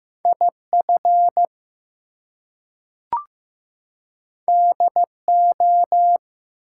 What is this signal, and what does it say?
Morse code, heard as a keyed tone, about 15 wpm, 700 Hz. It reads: IF DO
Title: Two Letter Words - 15wpm